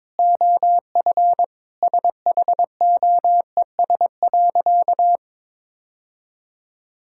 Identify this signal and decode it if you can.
Morse code, heard as a keyed tone, about 22 wpm, 700 Hz. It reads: OF SHOES.
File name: practice / 22wpm / Sentences with 3-Word Phrases - 22wpm